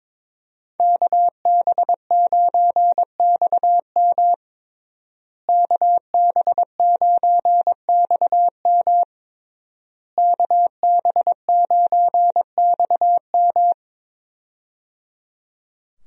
Morse code, heard as a keyed tone, about 22 wpm, 700 Hz. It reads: KB9XM KB9XM KB9XM
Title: USA Call Signs - 5 Characters - Repeated 3x - 22wpm